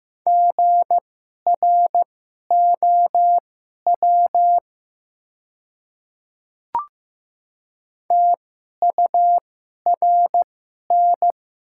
Morse code, heard as a keyed tone, about 15 wpm, spaced as for 11 wpm, 700 Hz. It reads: GROW TURN
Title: 2x Character Spacing - Top 300 Words - 15wpm